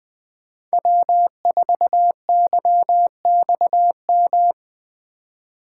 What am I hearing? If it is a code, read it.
Morse code, heard as a keyed tone, about 25 wpm, 700 Hz. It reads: W4YXM
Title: Speed Racing - USA 1x3 Call Signs 20wpm